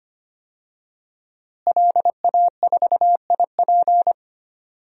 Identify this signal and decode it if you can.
Morse code, heard as a keyed tone, about 25 wpm, 700 Hz. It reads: LA4IP
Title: DXCC-100 Call Signs - 5 Characters - Repeated 2x - 25wpm